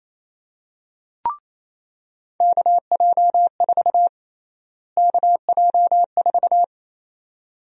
Morse code, heard as a keyed tone, about 27 wpm, 700 Hz. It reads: KJ4 KJ4
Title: USA Call Sign Prefixes - Repeated 2x - 28wpm